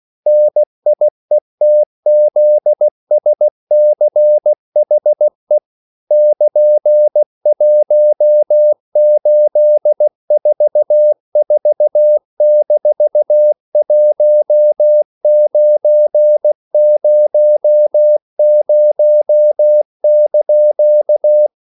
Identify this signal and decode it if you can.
Morse code, heard as a keyed tone, about 16 wpm, 600 Hz. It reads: NIETZSCHE <KN>1844-1900)
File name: CW / Practice / EnglishQuotations0216